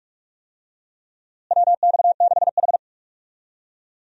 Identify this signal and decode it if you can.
Morse code, heard as a keyed tone, about 46 wpm, 700 Hz. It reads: WX6H
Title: USA Call Signs - 4 Characters - Repeated 2x - 45wpm